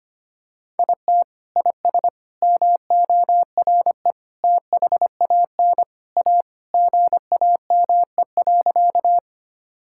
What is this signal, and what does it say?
Morse code, heard as a keyed tone, about 25 wpm, 700 Hz. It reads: IT IS MORE THAN A GAME.
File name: practice / 25wpm / Sentences with 3-Word Phrases - 25wpm